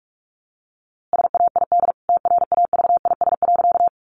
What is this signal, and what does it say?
Morse code, heard as a keyed tone, about 44 wpm, 700 Hz. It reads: SAID TRAVIS.
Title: A Sound Of Thunder 45wpm